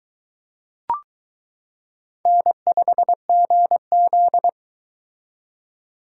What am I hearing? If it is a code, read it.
Morse code, heard as a keyed tone, about 22 wpm, 700 Hz. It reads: N5GZ